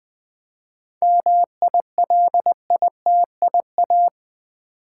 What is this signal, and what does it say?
Morse code, heard as a keyed tone, about 20 wpm, 700 Hz. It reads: MILITIA
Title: 7 Letter Words - First 3 Letters Given - 20wpm